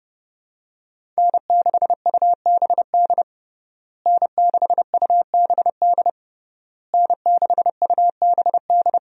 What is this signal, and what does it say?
Morse code, heard as a keyed tone, about 30 wpm, 700 Hz. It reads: N6UBD N6UBD N6UBD
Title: USA Call Signs - 5 Characters - Repeated 3x - 30wpm